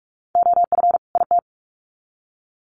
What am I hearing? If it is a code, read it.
Morse code, heard as a keyed tone, about 44 wpm, 700 Hz. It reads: OF IT
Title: Sets of 2 Words 45wpm